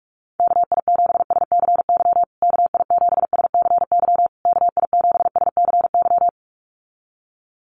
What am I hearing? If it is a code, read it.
Morse code, heard as a keyed tone, about 44 wpm, 700 Hz. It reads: KI7SCY KI7SCY KI7SCY
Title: USA Call Signs Repeated 3x 45wpm